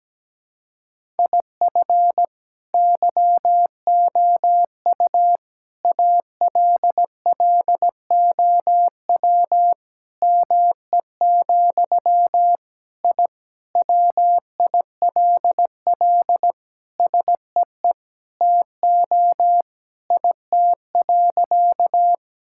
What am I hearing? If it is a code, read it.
Morse code, heard as a keyed tone, about 17 wpm, 700 Hz. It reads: IF YOU ALLOW ME, I WILL SEE TO IT.
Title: Sentences from Top 1000 Words - Review Part 5 - 17wpm